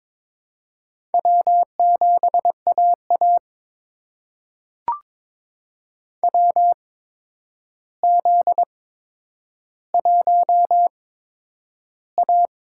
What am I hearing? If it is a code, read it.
Morse code, heard as a keyed tone, about 22 wpm, spaced as for 9 wpm, 700 Hz. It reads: W7AA WZ1A
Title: USA Call Signs - 4 Characters - 8x Character Spacing - 22wpm